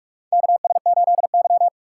Morse code, heard as a keyed tone, about 42 wpm, 700 Hz. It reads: KS8Y
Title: USA Call Signs - 4 Characters - 2x Character Spacing - 45wpm